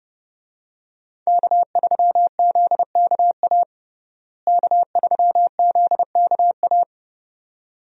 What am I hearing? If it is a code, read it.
Morse code, heard as a keyed tone, about 32 wpm, 700 Hz. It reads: K3ZKA K3ZKA
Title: USA Call Signs - 5 Characters - Repeated 2x - 30wpm